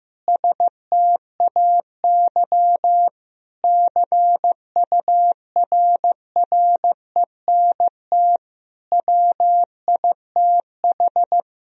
Morse code, heard as a keyed tone, about 15 wpm, 700 Hz. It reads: STAY CURRENT WITH
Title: Sentences from Top 1000 Words - Review Part 4 - 15wpm